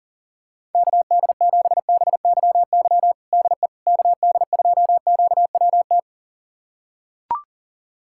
Morse code, heard as a keyed tone, about 40 wpm, 700 Hz. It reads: KD7BYY DE KD2QWT
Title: USA Callsign DE Callsign - 6 Characters - 40wpm